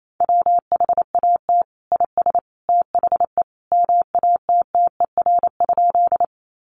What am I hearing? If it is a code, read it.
Morse code, heard as a keyed tone, about 28 wpm, 700 Hz. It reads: WHAT IS THE MATTER?
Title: Sentences from Top 700 Words 28wpm